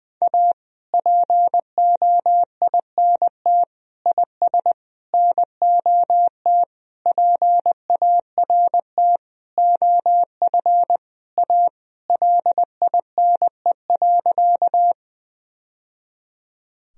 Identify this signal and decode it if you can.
Morse code, heard as a keyed tone, about 20 wpm, 700 Hz. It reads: A POINT IS NOT PART OF A LINE.
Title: Sentences from Top 1000 Words - Review Part 1 - 20wpm